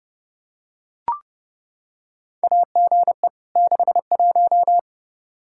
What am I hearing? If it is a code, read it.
Morse code, heard as a keyed tone, about 30 wpm, 700 Hz. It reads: AGE 61
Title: Practice QSO - Exchange 3 - More Info - 30wpm